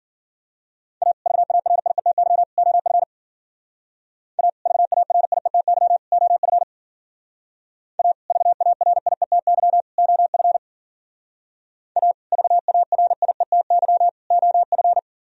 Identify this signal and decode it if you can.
Morse code, heard as a keyed tone, about 51 wpm, 700 Hz. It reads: A VARIETY OF A VARIETY OF A VARIETY OF A VARIETY OF